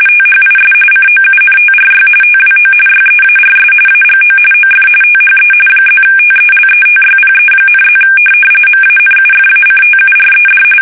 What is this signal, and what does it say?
RTTY, about 45.45 baud. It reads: THIS COUNTRY TWO YEARS AGO OF THE +TELETYPE,+ MANUFACTURED